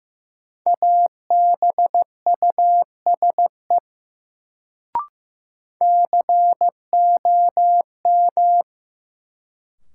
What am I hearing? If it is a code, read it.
Morse code, heard as a keyed tone, about 15 wpm, 700 Hz. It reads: ABUSE COM
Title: Sentences from Top 1300 Words - New Word Drill - 15wpm